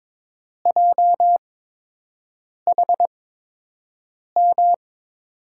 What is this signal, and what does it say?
Morse code, heard as a keyed tone, about 22 wpm, spaced as for 6 wpm, 700 Hz. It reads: JHM
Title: USA Call Signs - 6 Characters - 8x Character Spacing - 22wpm